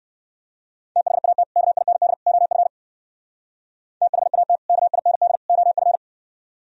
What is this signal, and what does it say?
Morse code, heard as a keyed tone, about 56 wpm, 700 Hz. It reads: THAT KIND OF THAT KIND OF